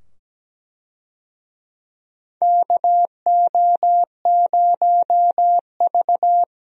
Morse code, heard as a keyed tone, about 17 wpm, 700 Hz. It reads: KO0V
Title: USA Call Signs - 4 Characters - 17wpm